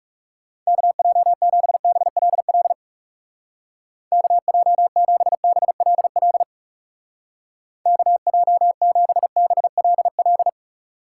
Morse code, heard as a keyed tone, about 35 wpm, 700 Hz. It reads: KJ7BLL KJ7BLL KJ7BLL
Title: Speed Racing - USA 2x3 Call Signs 30wpm